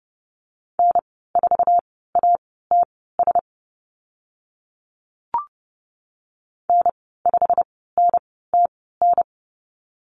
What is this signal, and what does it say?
Morse code, heard as a keyed tone, about 30 wpm, spaced as for 17 wpm, 700 Hz. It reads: N4ATS N5NTN
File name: Lesson 15g - Number 5 - Call Signs 30wpm